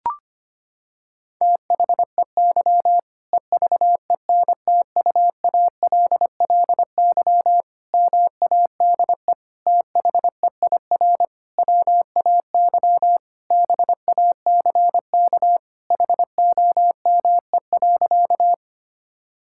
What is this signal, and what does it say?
Morse code, heard as a keyed tone, about 25 wpm, 700 Hz. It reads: THEY EVENTUALLY MADE THEIR WAY BACK HOME.
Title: Sentences from Top 1000 Words - Review Part 6 - 25wpm